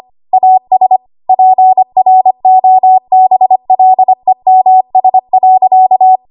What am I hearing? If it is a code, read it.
Morse code, heard as a keyed tone, about 25 wpm, 750 Hz. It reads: AS PROBLEMS.
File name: Archive / 25 WPM / 25WPM